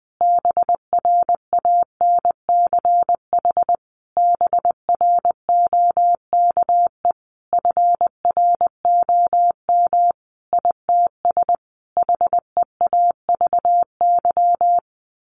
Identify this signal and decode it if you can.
Morse code, heard as a keyed tone, about 20 wpm, 700 Hz. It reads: BRANCH BROKE FROM ITS HEAVY
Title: A Sound Of Thunder 20wpm